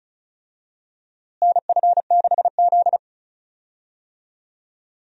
Morse code, heard as a keyed tone, about 36 wpm, 700 Hz. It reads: NF6Z
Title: USA Call Signs - 4 Characters - Repeated 3x - 35wpm